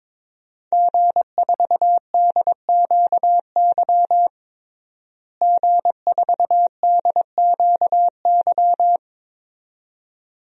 Morse code, heard as a keyed tone, about 22 wpm, 700 Hz. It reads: G4DQY G4DQY
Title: DXCC-100 Call Signs - 5 Characters - Repeated 2x - 22wpm